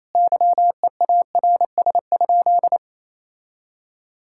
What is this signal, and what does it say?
Morse code, heard as a keyed tone, about 28 wpm, 700 Hz. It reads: YEARS?
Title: Sentences from Top 1000 Words - Review Part 6 - 28wpm